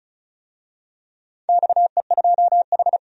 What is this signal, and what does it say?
Morse code, heard as a keyed tone, about 36 wpm, 700 Hz. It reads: XE2H
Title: DXCC-100 Call Signs - 4 Characters - 35wpm